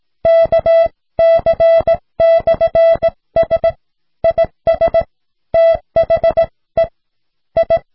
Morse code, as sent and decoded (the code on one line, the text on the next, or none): -.- -.-. -..-. ... / .. ... / - .... / ..
KC/S IS THE I